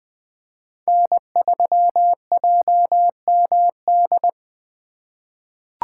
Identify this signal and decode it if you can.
Morse code, heard as a keyed tone, about 20 wpm, 700 Hz. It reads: N3JMD
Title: USA Call Signs - 5 Characters - 3x Character Spacing - 20wpm